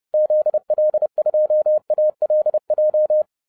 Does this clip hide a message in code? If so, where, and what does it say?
Yes: Morse code from 0.1 to 3.2 s, heard as a keyed tone, about 30 wpm, 610 Hz. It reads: ZL2ALJ